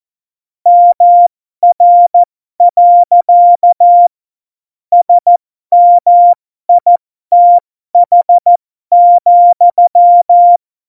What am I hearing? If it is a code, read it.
Morse code, heard as a keyed tone, about 14 wpm, spaced as for 12 wpm, 700 Hz. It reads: MR. SMITH,
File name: Morsekurs / Witze